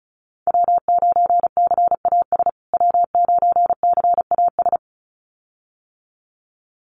Morse code, heard as a keyed tone, about 34 wpm, 700 Hz. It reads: W9CAS W9CAS
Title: USA Call Signs Repeated 2x 35wpm